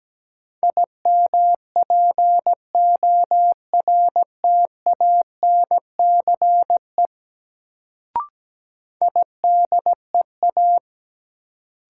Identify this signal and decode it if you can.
Morse code, heard as a keyed tone, about 17 wpm, 700 Hz. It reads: IMPORTANCE IDEA
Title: Sets of 3 Words - New Word Drill - 17wpm